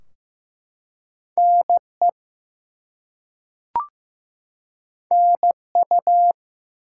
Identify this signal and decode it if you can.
Morse code, heard as a keyed tone, about 15 wpm, 700 Hz. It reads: NE NU
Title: US States and Canadian Provinces 15wpm